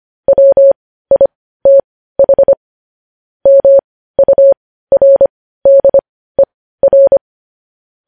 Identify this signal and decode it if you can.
Morse code, heard as a keyed tone, about 25 wpm, spaced as for 15 wpm, 550 Hz. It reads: WITH MURDER
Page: News Headlines in Morse Code at 15 WPM